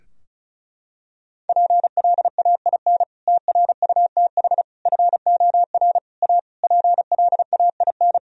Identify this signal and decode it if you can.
Morse code, heard as a keyed tone, about 34 wpm, 700 Hz. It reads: PLAIN TRUTH FOR A PLAIN
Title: Sentences from Top 1000 Words - Review Part 2 - 35wpm